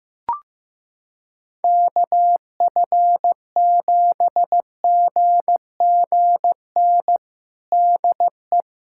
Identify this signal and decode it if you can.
Morse code, heard as a keyed tone, about 15 wpm, 700 Hz. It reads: KF7GGN DE
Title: USA Callsign DE Callsign 15wpm